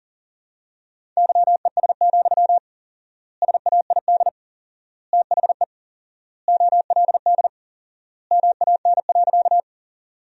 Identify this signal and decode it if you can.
Morse code, heard as a keyed tone, about 40 wpm, 700 Hz. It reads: YES, SAID THE OLD MAN.
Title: 4x Word-Spacing - Sentences from Top 500 Words - 40wpm